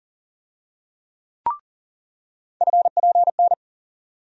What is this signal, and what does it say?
Morse code, heard as a keyed tone, about 37 wpm, 700 Hz. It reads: FPN
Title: USA Call Sign Postfixes - 3 Letters 40wpm